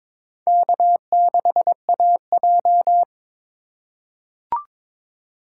Morse code, heard as a keyed tone, about 22 wpm, 700 Hz. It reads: K6AJ
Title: USA Call Signs - 4 Characters - 4x Character Spacing - 22wpm